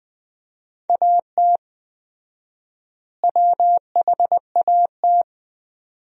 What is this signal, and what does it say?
Morse code, heard as a keyed tone, about 20 wpm, 700 Hz. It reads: AT WHAT
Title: 4x Word-Spacing - Sentences from Top 100 Words - 20wpm